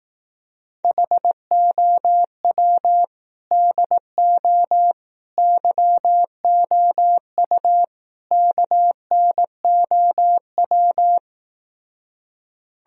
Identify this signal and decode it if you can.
Morse code, heard as a keyed tone, about 18 wpm, 700 Hz. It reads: HOW DO YOU KNOW
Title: Speed-Racing - Sets of 4 Words 15wpm